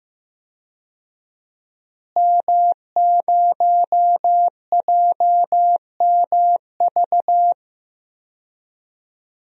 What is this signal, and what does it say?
Morse code, heard as a keyed tone, about 15 wpm, 700 Hz. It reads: M0JMV
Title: DXCC-100 Call Signs - 5 Characters - Repeated 3x - 15wpm